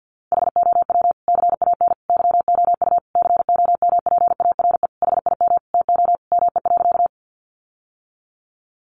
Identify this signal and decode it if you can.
Morse code, heard as a keyed tone, about 50 wpm, 700 Hz. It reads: HOW CAN YOU COMPARE HIM TO ME.
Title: Sentences from Top 1300 Words 50wpm